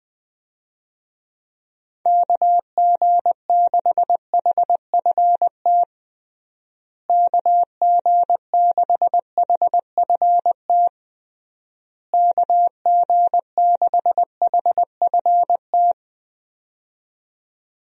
Morse code, heard as a keyed tone, about 20 wpm, 700 Hz. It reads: KG6HFT KG6HFT KG6HFT
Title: USA Call Signs - 6 Characters - Repeated 3x - 20wpm